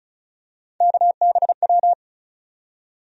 Morse code, heard as a keyed tone, about 36 wpm, 700 Hz. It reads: KBW
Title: USA Call Sign Postfixes - 3 Letters 35wpm